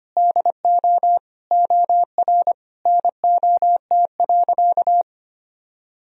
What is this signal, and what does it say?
Morse code, heard as a keyed tone, about 25 wpm, 700 Hz. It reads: DO OR NOT.
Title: Sentences from Top 1000 Words - Review Part 3 - 25wpm